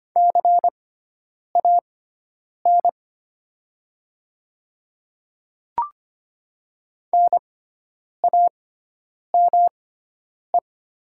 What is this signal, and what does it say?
Morse code, heard as a keyed tone, about 25 wpm, spaced as for 9 wpm, 700 Hz. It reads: CAN NAME
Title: 6x Character Spacing - Top 300 Words - 25wpm